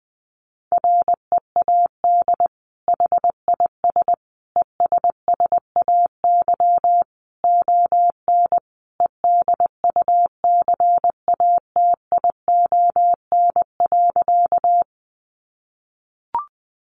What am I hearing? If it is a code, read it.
Morse code, heard as a keyed tone, about 20 wpm, 700 Hz. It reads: READ HIS ESSAY ON EDUCATION.
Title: Sentences from Top 1500 Words 20wpm